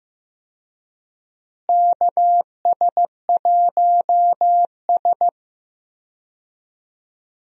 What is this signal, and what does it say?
Morse code, heard as a keyed tone, about 15 wpm, 700 Hz. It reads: KS1S
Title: USA Call Signs - 4 Characters - Repeated 2x - 15wpm